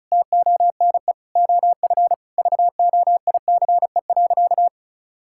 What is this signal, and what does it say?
Morse code, heard as a keyed tone, about 34 wpm, 700 Hz. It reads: TONE OF VOICE.
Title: Sentences from Top 1600 Words 35wpm